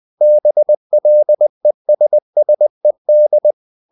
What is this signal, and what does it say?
Morse code, heard as a keyed tone, about 20 wpm, 600 Hz. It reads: BLESSED
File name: the old man and the sea / the old man and the sea part 1 20wpm